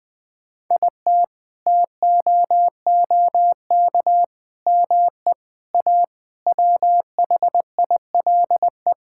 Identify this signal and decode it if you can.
Morse code, heard as a keyed tone, about 20 wpm, 700 Hz. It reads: IT TOOK ME A WHILE
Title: Sets of 8 Words - 20wpm